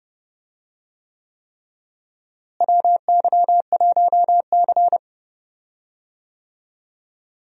Morse code, heard as a keyed tone, about 30 wpm, 700 Hz. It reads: WY1C
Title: USA Call Signs - 4 Characters - Repeated 3x - 30wpm